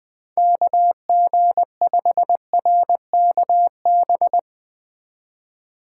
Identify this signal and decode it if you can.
Morse code, heard as a keyed tone, about 20 wpm, 700 Hz. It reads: KG5RKB
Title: USA Call Signs - 6 Characters - 4x Character Spacing - 20wpm